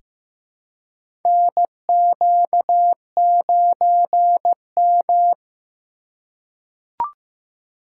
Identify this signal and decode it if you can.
Morse code, heard as a keyed tone, about 15 wpm, 700 Hz. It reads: NQ9M
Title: USA Call Signs - 4 Characters - 2x Character Spacing - 15wpm